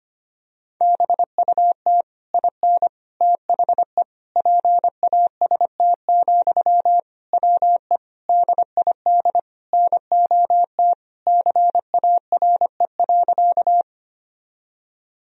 Morse code, heard as a keyed tone, about 25 wpm, 700 Hz. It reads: BUT IN THE PAST, WE DID NOT CARE.
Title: Sentences from Top 1000 Words - Review Part 3 - 25wpm